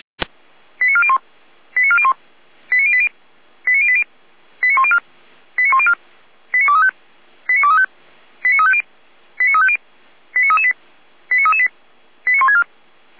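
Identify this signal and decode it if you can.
Five-tone selective call (ZVEI), heard as quick runs of five tones, tones of about 70 ms. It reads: 89401 89401 89080 89080 89104 89104 89235 89235 89350 89350 89338 89338 89164
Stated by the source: ZVEI five-tone call 89340 received by SDR.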